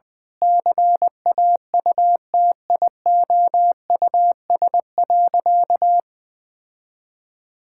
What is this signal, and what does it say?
Morse code, heard as a keyed tone, about 20 wpm, 700 Hz. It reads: CAUTIOUS.
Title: Sentences with 8-Word Phrases - Spoken Beginning - 20wpm